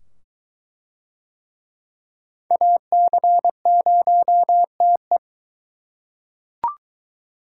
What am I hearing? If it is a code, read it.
Morse code, heard as a keyed tone, about 22 wpm, 700 Hz. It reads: AC0TE